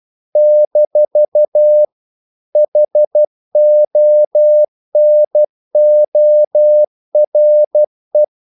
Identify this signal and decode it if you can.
Morse code, heard as a keyed tone, about 12 wpm, 600 Hz. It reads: - HONORE
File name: CW / Practice / EnglishQuotations0312